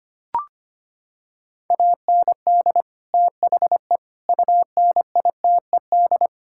Morse code, heard as a keyed tone, about 25 wpm, 700 Hz. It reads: AND THE UNITED
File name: practice / 25wpm / Sentences with 4-Word Phrases - 25wpm